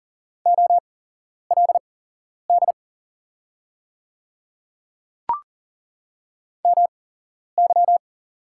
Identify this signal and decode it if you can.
Morse code, heard as a keyed tone, about 40 wpm, spaced as for 11 wpm, 700 Hz. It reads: OLD MY
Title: 8x Character Spacing - Top 200 Words - 40wpm